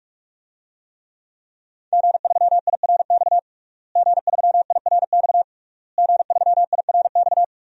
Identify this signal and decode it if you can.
Morse code, heard as a keyed tone, about 47 wpm, 700 Hz. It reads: G3IRX G3IRX G3IRX
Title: DXCC-100 Call Signs - 5 Characters - Repeated 3x - 45wpm